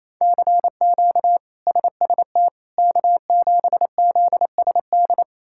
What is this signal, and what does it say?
Morse code, heard as a keyed tone, about 28 wpm, 700 Hz. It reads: CQ SST K7ZSD
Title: K1USN SST QSO Practice - 28wpm